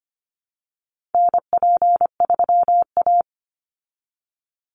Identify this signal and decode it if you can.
Morse code, heard as a keyed tone, about 25 wpm, 700 Hz. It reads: NP3A
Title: Call Signs Repeated 3x - 25wpm